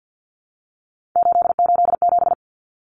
Morse code, heard as a keyed tone, about 50 wpm, 700 Hz. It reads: 887
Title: Common Cut Numbers - 3-Digit 50wpm